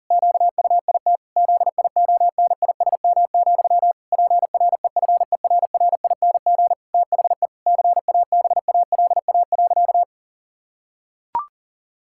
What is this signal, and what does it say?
Morse code, heard as a keyed tone, about 40 wpm, 700 Hz. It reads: QUIT ZIONISM, PREFERRING THE CABALA.
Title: Pangram Challenge v2 - 40wpm